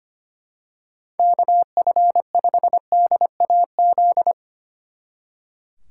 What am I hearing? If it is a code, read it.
Morse code, heard as a keyed tone, about 27 wpm, 700 Hz. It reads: KF5DAZ
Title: Speed Racing - USA 2x3 Call Signs 25wpm